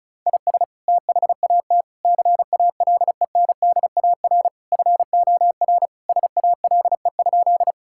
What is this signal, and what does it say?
Morse code, heard as a keyed tone, about 34 wpm, 700 Hz. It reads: IS THAT CALENDAR FOR SALE?
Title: Sentences from Top 1600 Words 35wpm